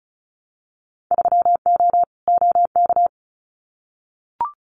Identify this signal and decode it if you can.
Morse code, heard as a keyed tone, about 34 wpm, 700 Hz. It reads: 3O OK
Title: Winter Field Day Exchanges 35wpm